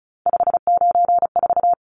Morse code, heard as a keyed tone, about 34 wpm, 700 Hz. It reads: 594
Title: Numbers - 3-Digit 35wpm